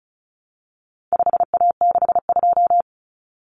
Morse code, heard as a keyed tone, about 34 wpm, 700 Hz. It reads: 5A62
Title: Common Cut Numbers - 4-Digit 35wpm